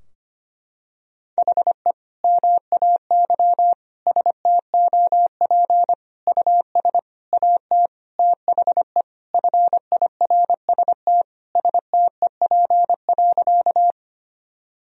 Morse code, heard as a keyed tone, about 25 wpm, 700 Hz. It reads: HE MAY STOP US AT THE FIRST STEP.
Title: Sentences from Top 1000 Words - Review Part 2 - 25wpm